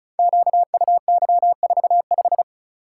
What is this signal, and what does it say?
Morse code, heard as a keyed tone, about 34 wpm, 700 Hz. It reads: QUY45
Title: Code Groups - 5 Letter:Numbers - 35wpm